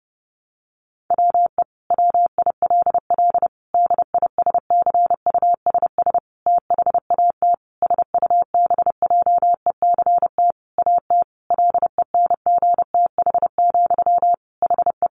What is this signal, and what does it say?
Morse code, heard as a keyed tone, about 30 wpm, 700 Hz. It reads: WE WILL DISCUSS THAT SUBJECT AT LENGTH, HE
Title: Sentences from Top 1000 Words 30wpm